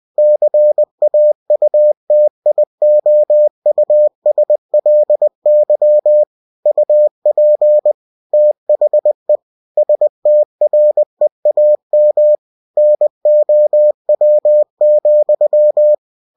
Morse code, heard as a keyed tone, about 20 wpm, 600 Hz. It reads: CAUTIOUSLY UP THE STREAM NOW,
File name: winnie-20wpm